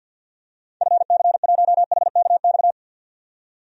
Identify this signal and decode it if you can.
Morse code, heard as a keyed tone, about 52 wpm, 700 Hz. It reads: FC1HKX
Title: DXCC-100 Call Signs - 6 Characters - 50wpm